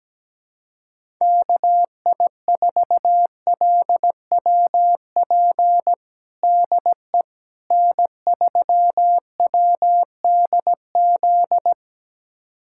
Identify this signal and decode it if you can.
Morse code, heard as a keyed tone, about 17 wpm, 700 Hz. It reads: KI4LWP DE N3WDZ
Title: USA Callsign DE Callsign 17wpm